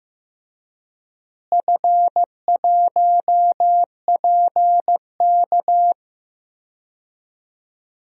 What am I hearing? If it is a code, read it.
Morse code, heard as a keyed tone, about 15 wpm, 700 Hz. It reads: F1PK
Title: DXCC-100 Call Signs - 4 Characters - Repeated 3x - 15wpm